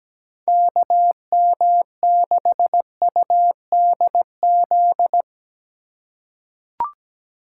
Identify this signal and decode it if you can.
Morse code, heard as a keyed tone, about 17 wpm, 700 Hz. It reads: KM6UDZ
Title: USA Call Signs - 3x Character Spacing - 17wpm